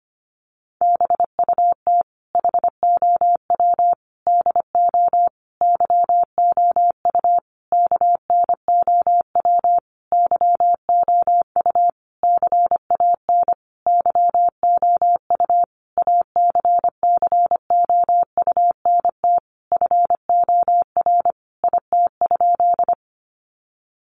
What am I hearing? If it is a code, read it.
Morse code, heard as a keyed tone, about 25 wpm, 700 Hz. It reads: BUT HOW DO YOU KNOW YOU CAN YOU ACCOUNT FOR IT?
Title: Sentences from Top 900 Words 25wpm